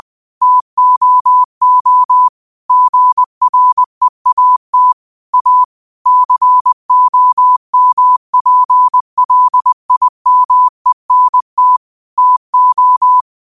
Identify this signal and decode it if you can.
Morse code, heard as a keyed tone, about 20 wpm, 1000 Hz. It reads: TOO GREAT A COMPLIMENT TO